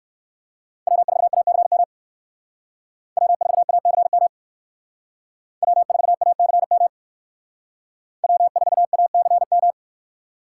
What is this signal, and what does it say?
Morse code, heard as a keyed tone, about 57 wpm, 700 Hz. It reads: W4ACM W4ACM W4ACM W4ACM